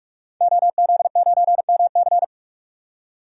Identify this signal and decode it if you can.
Morse code, heard as a keyed tone, about 41 wpm, 700 Hz. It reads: OZ9MC
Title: DXCC-100 Call Signs - 5 Characters - 45wpm